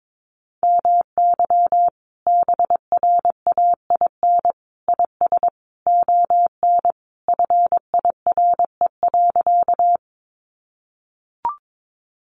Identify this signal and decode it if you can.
Morse code, heard as a keyed tone, about 22 wpm, 700 Hz. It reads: MY BRAIN IS ON FIRE.
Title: Sentences from Top 1100 Words 22wpm